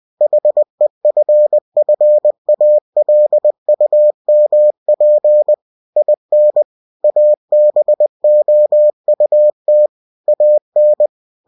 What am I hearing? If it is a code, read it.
Morse code, heard as a keyed tone, about 20 wpm, 600 Hz. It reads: HEFFALUMP IN ABOUT AN
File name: cw-books / winnie-the-pooh / winnie-20wpm-page-0041